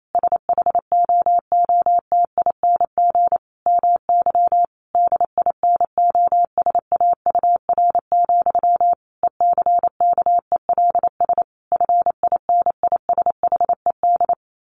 Morse code, heard as a keyed tone, about 28 wpm, 700 Hz. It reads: SHOOTING MY DINOSAUR, ECKELS FINISHED
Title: A Sound Of Thunder 28wpm